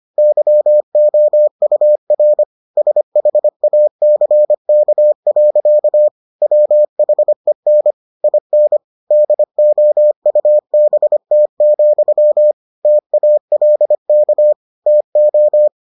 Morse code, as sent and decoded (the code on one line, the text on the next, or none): -.-- --- ..- .-. / ... .... .- -.-. -.- .-.-.- / .-- .... -. / .. -. / -.. --- ..- -... - --..-- / - .- .-.. -.- / - ---
YOUR SHACK. WHEN IN DOUBT, TALK TO